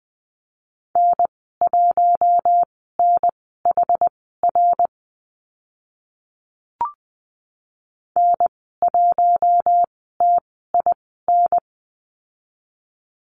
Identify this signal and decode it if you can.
Morse code, heard as a keyed tone, about 20 wpm, spaced as for 14 wpm, 700 Hz. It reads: N1NHR N1TIN
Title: 062 Lesson 11g - Letter h - Call Signs 20wpm